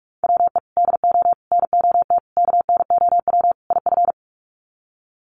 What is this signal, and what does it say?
Morse code, heard as a keyed tone, about 46 wpm, 700 Hz. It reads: WE DO NOT KNOW IF
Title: Sets of 5 Words Encore2 45wpm